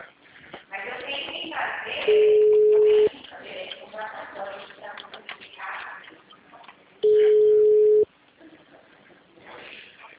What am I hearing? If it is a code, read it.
Ringback tone (Europe, 425 Hz), with speech.